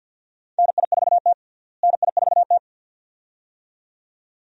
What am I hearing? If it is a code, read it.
Morse code, heard as a keyed tone, about 52 wpm, 700 Hz. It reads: NI4T NI4T
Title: USA Call Signs - 4 Characters - Repeated 2x - 50wpm